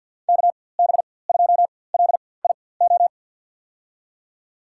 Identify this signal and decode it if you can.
Morse code, heard as a keyed tone, about 53 wpm, spaced as for 24 wpm, 700 Hz. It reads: KB2LIO
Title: USA Call Signs - 6 Characters - 4x Character Spacing - 50wpm